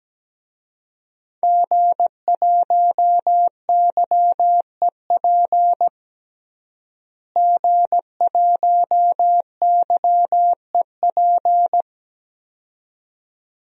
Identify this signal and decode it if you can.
Morse code, heard as a keyed tone, about 17 wpm, 700 Hz. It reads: G1YEP G1YEP
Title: DXCC-100 Call Signs - 5 Characters - Repeated 2x - 17wpm